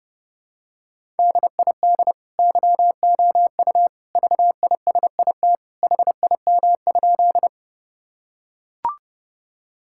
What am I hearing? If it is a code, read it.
Morse code, heard as a keyed tone, about 32 wpm, 700 Hz. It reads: DID YOU VISIT HIM?
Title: Sentences from Top 1000 Words - Review Part 6 - 30wpm